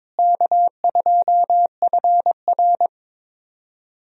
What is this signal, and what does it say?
Morse code, heard as a keyed tone, about 22 wpm, 700 Hz. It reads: K2FR
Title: USA Call Signs - 4 Characters - Repeated 2x - 22wpm